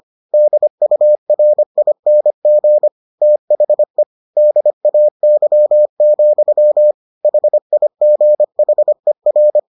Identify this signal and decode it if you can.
Morse code, heard as a keyed tone, about 25 wpm, 600 Hz. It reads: DURING THE DAY, HIGHER